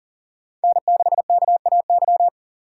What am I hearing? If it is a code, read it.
Morse code, heard as a keyed tone, about 52 wpm, 700 Hz. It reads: N6KAY